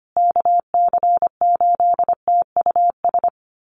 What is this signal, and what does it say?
Morse code, heard as a keyed tone, about 25 wpm, 700 Hz. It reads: KC8TUS
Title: USA Call Signs Repeated 2x 25wpm